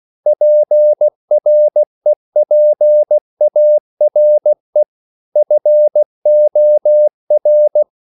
Morse code, heard as a keyed tone, about 16 wpm, 600 Hz. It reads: PREPARE FOR
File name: CW/Practice